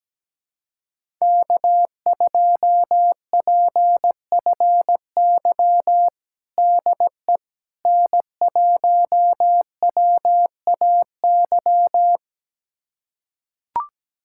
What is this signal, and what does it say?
Morse code, heard as a keyed tone, about 17 wpm, 700 Hz. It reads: K2PFY DE N1WAY
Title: USA Callsign DE Callsign 17wpm